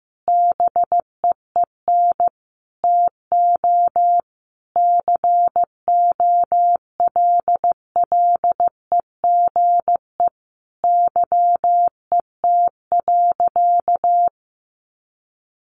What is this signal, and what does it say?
Morse code, heard as a keyed tone, about 15 wpm, 700 Hz. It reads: BEEN TO COLLEGE YET.
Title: Sentences from Top 1300 Words 15wpm